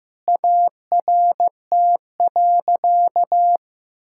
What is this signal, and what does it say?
Morse code, heard as a keyed tone, about 15 wpm, 700 Hz. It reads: ART.
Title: Sentences from Top 1000 Words - Review Part 6 - 15wpm